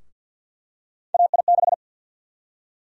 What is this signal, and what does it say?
Morse code, heard as a keyed tone, about 53 wpm, 700 Hz. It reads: AI6
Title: USA Call Sign Prefixes - 50wpm